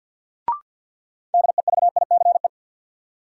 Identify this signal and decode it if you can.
Morse code, heard as a keyed tone, about 49 wpm, 700 Hz. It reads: DEVICE
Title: Sentences from Top 1500 Words - New Word Drill - 50wpm